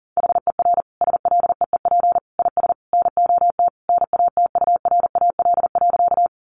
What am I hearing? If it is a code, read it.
Morse code, heard as a keyed tone, about 40 wpm, 700 Hz. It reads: HER SLEEP IS NOT NATURAL.
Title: Sentences from Top 700 Words 40wpm